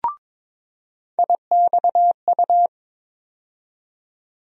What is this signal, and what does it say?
Morse code, heard as a keyed tone, about 22 wpm, 700 Hz. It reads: IXU
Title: USA Call Sign Postfixes - 3 Letters 22wpm